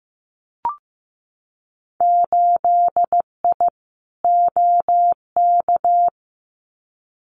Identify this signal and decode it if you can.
Morse code, heard as a keyed tone, about 15 wpm, 700 Hz. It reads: 8I OK
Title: Winter Field Day Exchanges Encore 15wpm